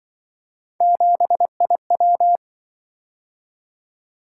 Morse code, heard as a keyed tone, about 25 wpm, 700 Hz. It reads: 7IW